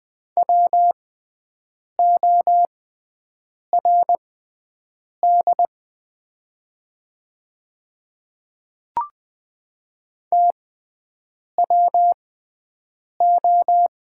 Morse code, heard as a keyed tone, about 20 wpm, spaced as for 7 wpm, 690 Hz. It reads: WORD TWO
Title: 6x Character Spacing - Top 100 Words - 20wpm